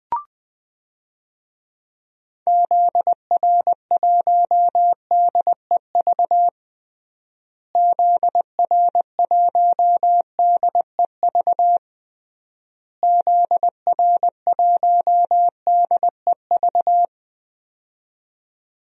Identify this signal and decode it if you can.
Morse code, heard as a keyed tone, about 20 wpm, 700 Hz. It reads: ZR1DEV ZR1DEV ZR1DEV
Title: DXCC-100 Call Signs - 6 Characters - Repeated 3x - 20wpm